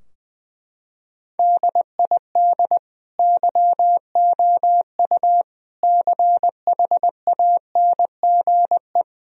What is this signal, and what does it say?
Morse code, heard as a keyed tone, about 21 wpm, 700 Hz. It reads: DID YOU CHANGE
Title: Sentences from Top 1000 Words - Review Part 2 - 20wpm